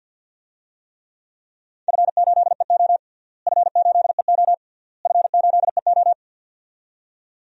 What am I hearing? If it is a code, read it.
Morse code, heard as a keyed tone, about 50 wpm, 700 Hz. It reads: F8EO F8EO F8EO
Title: DXCC-100 Call Signs - 4 Characters - Repeated 3x - 50wpm